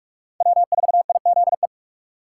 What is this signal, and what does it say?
Morse code, heard as a keyed tone, about 47 wpm, 700 Hz. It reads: W4IZE
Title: USA Call Signs - 5 Characters - 2x Character Spacing - 45wpm